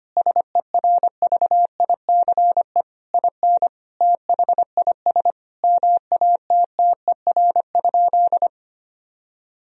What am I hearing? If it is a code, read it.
Morse code, heard as a keyed tone, about 25 wpm, 700 Hz. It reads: SERVICE IN THIS MATTER?
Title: Sentences from Top 1000 Words - Review Part 4 - 25wpm